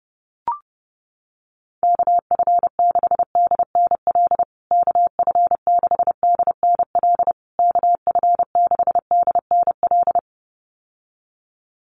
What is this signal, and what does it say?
Morse code, heard as a keyed tone, about 30 wpm, 700 Hz. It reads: KF6DNL KF6DNL KF6DNL
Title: USA Call Signs Repeated 3x 30wpm